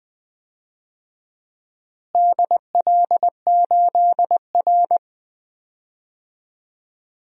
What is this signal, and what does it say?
Morse code, heard as a keyed tone, about 20 wpm, 700 Hz. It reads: DL8R
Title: DXCC-100 Call Signs - 4 Characters - Repeated 2x - 20wpm